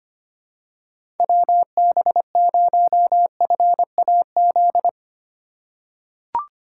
Morse code, heard as a keyed tone, about 25 wpm, 700 Hz. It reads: WB0FAZ
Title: USA Call Signs - 3x Character Spacing - 25wpm